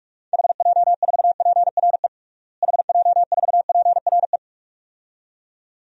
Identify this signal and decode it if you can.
Morse code, heard as a keyed tone, about 46 wpm, 700 Hz. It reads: HJ4PRE HJ4PRE
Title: DXCC-100 Call Signs - 6 Characters - Repeated 2x - 45wpm